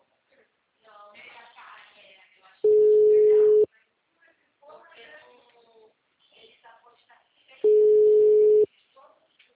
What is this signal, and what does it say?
Ringback tone (Europe, 425 Hz), with speech.
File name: out- -2034- - - .697